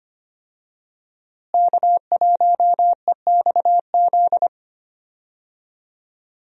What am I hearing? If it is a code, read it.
Morse code, heard as a keyed tone, about 25 wpm, 700 Hz. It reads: K1EXZ
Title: USA Call Signs - 5 Characters - Repeated 3x - 25wpm